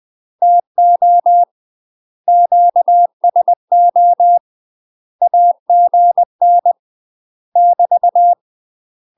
Morse code, heard as a keyed tone, about 20 wpm, 700 Hz. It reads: TO QSO AGN <BT>
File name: UK MASTER QSOs TRAINING / 20WPM / UK MASTER QSOs 0039